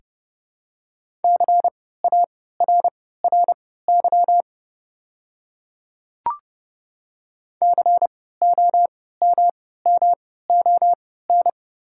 Morse code, heard as a keyed tone, about 30 wpm, spaced as for 17 wpm, 700 Hz. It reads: CARRY COMMON
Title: 3x Character Spacing - Top 500 Words - 30wpm